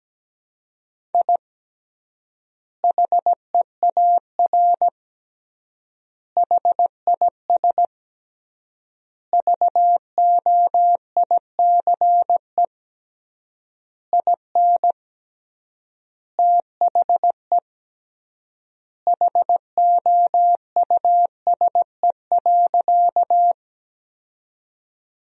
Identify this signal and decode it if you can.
Morse code, heard as a keyed tone, about 17 wpm, 700 Hz. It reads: I HEAR HIS VOICE IN THE HOUSE.
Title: 3x Word-Spacing - Sentences from Top 500 Words - 17wpm